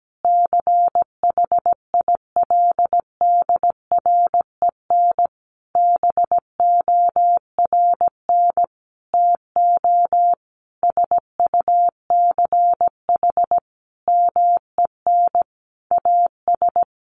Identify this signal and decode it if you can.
Morse code, heard as a keyed tone, about 17 wpm, 700 Hz. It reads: CHILDREN BORN TO SUCH MEN AS
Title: Sentences from Top 1000 Words 17wpm